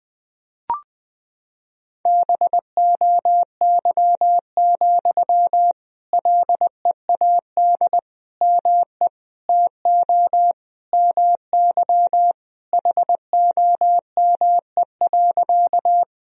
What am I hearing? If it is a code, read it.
Morse code, heard as a keyed tone, about 20 wpm, 700 Hz. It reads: BOY, LEAD ME TO MY HOME.
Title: Sentences from Top 1000 Words - Review Part 2 - 20wpm